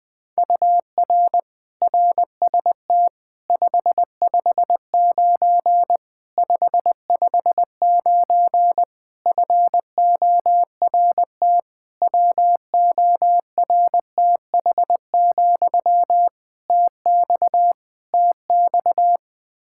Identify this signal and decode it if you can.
Morse code, heard as a keyed tone, about 21 wpm, 700 Hz. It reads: UR RST 559 559 FORT WORTH, TX TX
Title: Practice QSO - Exchange 1 - Beginning - 20wpm